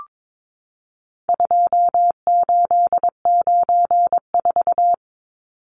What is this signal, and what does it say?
Morse code, heard as a keyed tone, about 22 wpm, 700 Hz. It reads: 2894
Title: Numbers - 4-Digit 22wpm